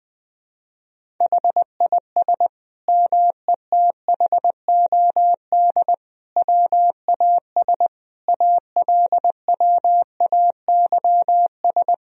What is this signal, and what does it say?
Morse code, heard as a keyed tone, about 20 wpm, 700 Hz. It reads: HIS METHOD WAS ALWAYS
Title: Sentences from Top 1000 Words - Review Part 3 - 20wpm